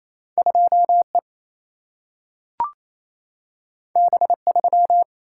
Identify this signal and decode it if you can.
Morse code, heard as a keyed tone, about 28 wpm, 700 Hz. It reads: 2E B3
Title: Single-Letter Single-Number 28wpm